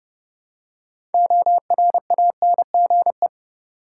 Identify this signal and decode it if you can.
Morse code, heard as a keyed tone, about 30 wpm, 700 Hz. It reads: ORANGE
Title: Six Letter Words - 30wpm